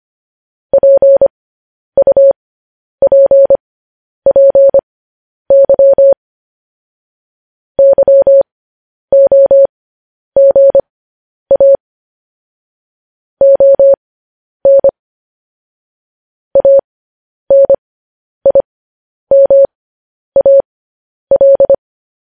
Morse code, heard as a keyed tone, about 25 wpm, spaced as for 10 wpm, 550 Hz. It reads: PUPPY YOGA ON ANIMAL
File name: podcast / News headlines-1x-25x10